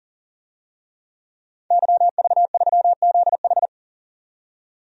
Morse code, heard as a keyed tone, about 40 wpm, 700 Hz. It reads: YV3ZH
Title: DXCC-100 Call Signs - 5 Characters - 40wpm